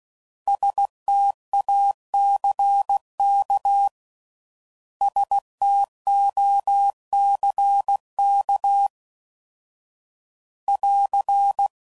Morse code, heard as a keyed tone, about 16 wpm, 800 Hz. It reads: STACK STOCK <AR>
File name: Prefixes / By First Letter / Ss